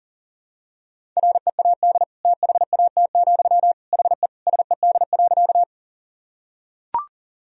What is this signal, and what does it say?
Morse code, heard as a keyed tone, about 40 wpm, 700 Hz. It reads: READ THAT, HE SED.
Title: Sentences from Top 200 Words and Abbreviations 40wpm